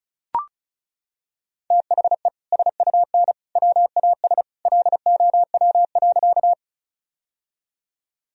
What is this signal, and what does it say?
Morse code, heard as a keyed tone, about 36 wpm, 690 Hz. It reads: THE SUN WAS LOW.
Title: Sentences from Top 1000 Words - Review Part 1 - 35wpm